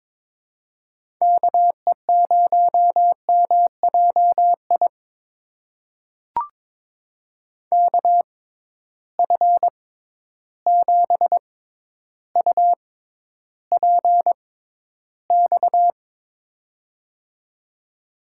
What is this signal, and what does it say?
Morse code, heard as a keyed tone, about 22 wpm, spaced as for 11 wpm, 700 Hz. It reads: KE0MJI KF7UPX
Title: USA Call Signs - 6 Characters - 6x Character Spacing - 22wpm